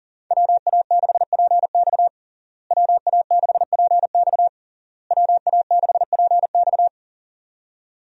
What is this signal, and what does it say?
Morse code, heard as a keyed tone, about 37 wpm, 700 Hz. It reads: WA6PX WA6PX WA6PX
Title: USA Call Signs - 5 Characters - Repeated 3x - 40wpm